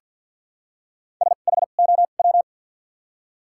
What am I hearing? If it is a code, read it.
Morse code, heard as a keyed tone, about 48 wpm, 700 Hz. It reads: S H O W
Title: 2x Character Spacing - Top 500 Words - 50wpm